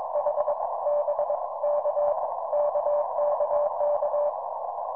Morse code, heard as a keyed tone, about 22 wpm, 590 Hz, with noise. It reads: H B K KKK